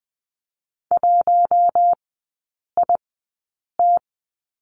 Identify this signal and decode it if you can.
Morse code, heard as a keyed tone, about 20 wpm, spaced as for 8 wpm, 700 Hz. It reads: 1IT
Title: 100 Lesson 17e - Letter u - Three Characters 20wpm